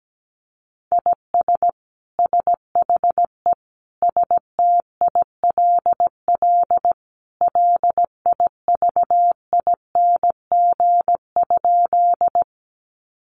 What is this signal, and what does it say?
Morse code, heard as a keyed tone, about 17 wpm, 700 Hz. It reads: IS SHE STILL LIVING?
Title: Sentences from Top 1100 Words 17wpm